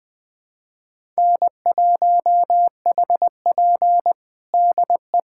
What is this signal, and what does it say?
Morse code, heard as a keyed tone, about 20 wpm, 700 Hz. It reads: N1HP DE
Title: USA Callsign DE Callsign - 4 Characters - 20wpm